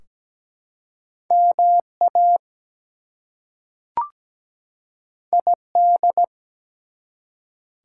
Morse code, heard as a keyed tone, about 17 wpm, 700 Hz. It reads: MA ID
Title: US States and Canadian Provinces 17wpm